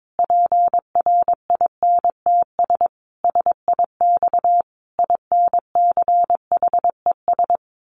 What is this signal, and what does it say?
Morse code, heard as a keyed tone, about 22 wpm, 700 Hz. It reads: PRINTS SIX INCHES
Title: A Sound Of Thunder 22wpm